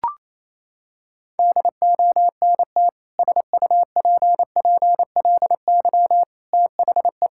Morse code, heard as a keyed tone, about 28 wpm, 700 Hz. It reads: DONT SUPPLY THE
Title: Sentences from Top 1000 Words - Review Part 5 - 28wpm